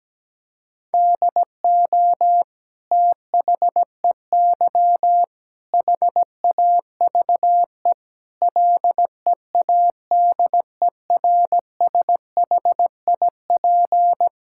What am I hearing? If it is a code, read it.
Morse code, heard as a keyed tone, about 17 wpm, 700 Hz. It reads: DO THEY HAVE LEADERSHIP
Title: Sentences from Top 1600 Words 17wpm